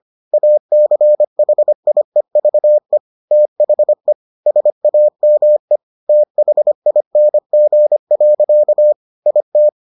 Morse code, heard as a keyed tone, about 25 wpm, 600 Hz. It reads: ACHIEVE THE SAME THING. IT